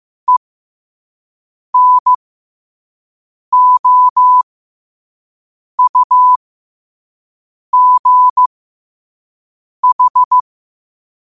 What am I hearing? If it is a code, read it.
Morse code, heard as a keyed tone, about 15 wpm, spaced as for 5 wpm, 1000 Hz. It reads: ENOUGH
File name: qotdfiles / 5wpm